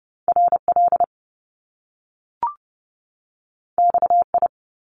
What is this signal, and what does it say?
Morse code, heard as a keyed tone, about 30 wpm, 700 Hz. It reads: R L X I
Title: Top Two Letter Combinations in Words 30wpm